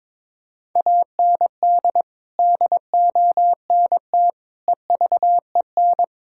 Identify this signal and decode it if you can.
Morse code, heard as a keyed tone, about 22 wpm, 700 Hz. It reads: AND DONT EVEN
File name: practice / 22wpm / Sentences with 8-Word Phrases - 22wpm